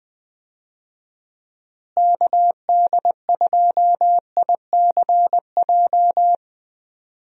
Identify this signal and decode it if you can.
Morse code, heard as a keyed tone, about 20 wpm, 700 Hz. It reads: KD2ICJ
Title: USA Call Signs - 6 Characters - Repeated 3x - 20wpm